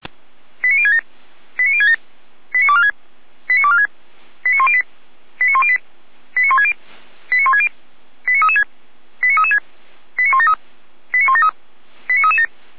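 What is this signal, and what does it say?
Five-tone selective call (ZVEI), heard as quick runs of five tones, tones of about 70 ms. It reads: 89067 89067 89246 89246 89108 89108 89160 89160 89336 89336 89173 89173 89337
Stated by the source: ZVEI five-tone call received by SDR.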